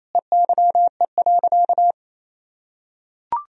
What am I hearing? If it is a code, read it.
Morse code, heard as a keyed tone, about 28 wpm, 700 Hz. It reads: EYE.
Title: Sentences from Top 1000 Words - Review Part 1 - 28wpm